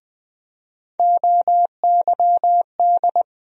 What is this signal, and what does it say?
Morse code, heard as a keyed tone, about 20 wpm, 700 Hz. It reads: OYD
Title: USA Call Sign Postfixes - 3 Letters 20wpm